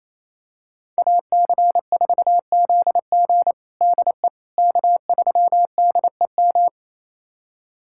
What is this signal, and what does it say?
Morse code, heard as a keyed tone, about 28 wpm, 700 Hz. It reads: AC4ZG DE K3DEM
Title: USA Callsign DE Callsign - 5 Characters - 28wpm